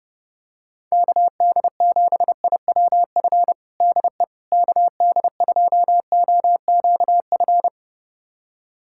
Morse code, heard as a keyed tone, about 32 wpm, 700 Hz. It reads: KD7IWF DE KD2OQF
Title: USA Callsign DE Callsign - 6 Characters - 30wpm